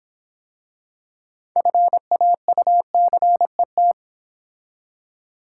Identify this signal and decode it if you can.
Morse code, heard as a keyed tone, about 26 wpm, 700 Hz. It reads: FAUCET